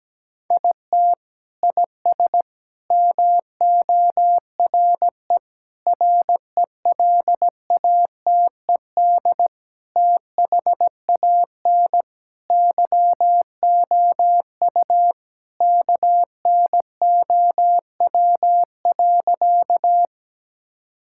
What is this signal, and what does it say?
Morse code, heard as a keyed tone, about 17 wpm, 700 Hz. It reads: IT IS MORE RELATED THAN YOU KNOW.
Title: Sentences from Top 1000 Words - Review Part 5 - 17wpm